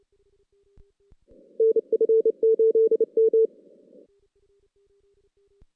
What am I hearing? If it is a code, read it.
Morse code, heard as a keyed tone, about 32 wpm, 450 Hz. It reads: NF8M
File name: 2016 JAN CW NAQP / 2016 Jan naqp k5ot - 0-5